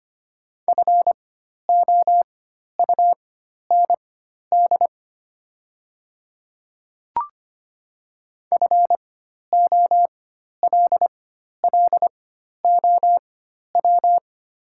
Morse code, heard as a keyed tone, about 25 wpm, spaced as for 12 wpm, 690 Hz. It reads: FOUND FOLLOW
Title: 4x Character Spacing - Top 500 Words - 25wpm